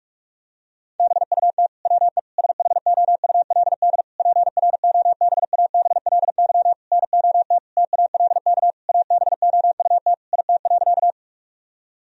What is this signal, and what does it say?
Morse code, heard as a keyed tone, about 46 wpm, 700 Hz. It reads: BUT WE SHOULD PROBABLY NOT TALK ABOUT IT.
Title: Sentences from Top 1000 Words - Review Part 4 - 45wpm